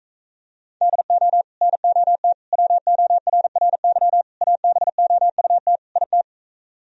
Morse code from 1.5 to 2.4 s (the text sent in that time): NOT